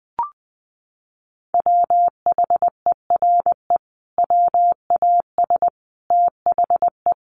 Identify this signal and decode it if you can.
Morse code, heard as a keyed tone, about 20 wpm, 700 Hz. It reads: WHERE WAS THE
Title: Sentences from Top 1500 Words 20wpm